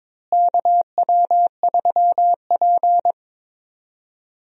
Morse code, heard as a keyed tone, about 23 wpm, 700 Hz. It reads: KW3P
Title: USA Call Signs - 4 Characters - Repeated 3x - 22wpm